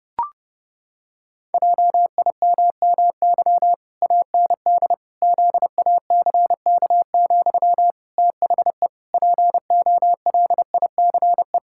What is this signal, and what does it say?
Morse code, heard as a keyed tone, about 32 wpm, 700 Hz. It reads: JIMMY AND ZACK, THE POLICE
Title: Pangram Challenge v2 - 30wpm